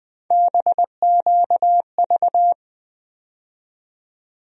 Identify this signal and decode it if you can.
Morse code, heard as a keyed tone, about 21 wpm, 700 Hz. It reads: BQV
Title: USA Call Sign Postfixes - 3 Letters 20wpm